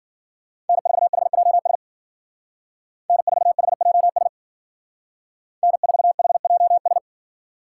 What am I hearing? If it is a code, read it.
Morse code, heard as a keyed tone, about 52 wpm, 700 Hz. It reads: N4HJS N4HJS N4HJS